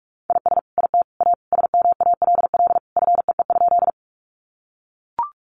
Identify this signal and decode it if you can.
Morse code, heard as a keyed tone, about 44 wpm, 700 Hz. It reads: IS IT A SMALL FEE?
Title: Sentences from Top 1500 Words 45wpm